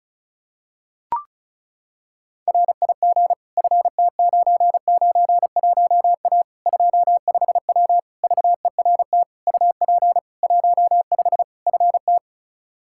Morse code, heard as a keyed tone, about 36 wpm, 700 Hz. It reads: RIG FT991A 25W VERT UP 15 FT
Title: Practice QSO - Exchange 2 - Info - 35wpm